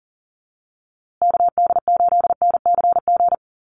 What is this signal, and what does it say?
Morse code, heard as a keyed tone, about 40 wpm, 700 Hz. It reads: KD8NCG
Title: USA Call Signs Repeated 3x 40wpm